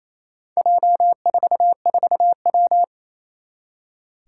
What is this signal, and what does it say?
Morse code, heard as a keyed tone, about 29 wpm, 700 Hz. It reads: J44W
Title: Four Challenging Letter-Number Combinations 28wpm